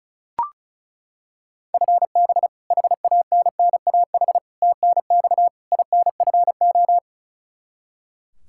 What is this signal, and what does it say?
Morse code, heard as a keyed tone, about 34 wpm, 700 Hz. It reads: FB HANNAH TNX INFO
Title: Practice QSO - Exchange 3 - More Info - 35wpm